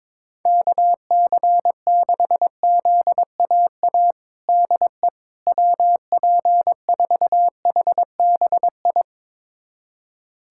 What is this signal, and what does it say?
Morse code, heard as a keyed tone, about 21 wpm, 700 Hz. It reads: KC6ZAA DE WP4HBI
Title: USA Callsign DE Callsign - 6 Characters - 22wpm